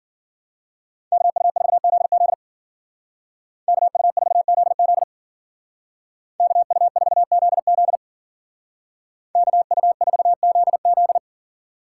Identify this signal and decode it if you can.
Morse code, heard as a keyed tone, about 49 wpm, 700 Hz. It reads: KU4ZZ KU4ZZ KU4ZZ KU4ZZ
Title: Speed Racing - USA Call Signs 40wpm